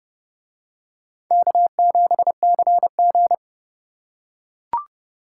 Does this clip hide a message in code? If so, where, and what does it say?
Yes: Morse code from 1.3 to 3.3 s, heard as a keyed tone, about 30 wpm, 700 Hz. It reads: K7CG